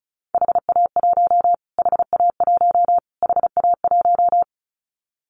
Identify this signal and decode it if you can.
Morse code, heard as a keyed tone, about 34 wpm, 700 Hz. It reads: HA1 HA1 HA1
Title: Callsign Prefixes for Top 25 DXCC - Repeated 3x - 35wpm